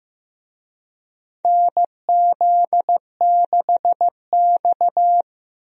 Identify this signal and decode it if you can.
Morse code, heard as a keyed tone, about 15 wpm, 700 Hz. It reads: NZ6X
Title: USA Call Signs - 4 Characters - Repeated 2x - 15wpm